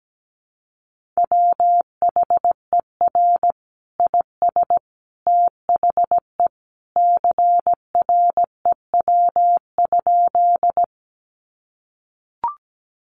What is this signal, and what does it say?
Morse code, heard as a keyed tone, about 17 wpm, 700 Hz. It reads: WHER IS THE CREW?
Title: Sentences from Top 1500 Words 17wpm